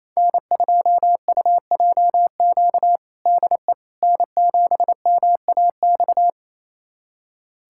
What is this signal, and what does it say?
Morse code, heard as a keyed tone, about 28 wpm, 700 Hz. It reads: N2UJQ DE N7MAX
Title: USA Callsign DE Callsign - 5 Characters - 28wpm